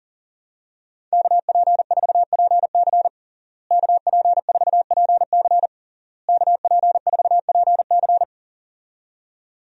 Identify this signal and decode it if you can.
Morse code, heard as a keyed tone, about 37 wpm, 700 Hz. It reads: KP4PC KP4PC KP4PC
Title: DXCC-100 Call Signs - 5 Characters - Repeated 3x - 40wpm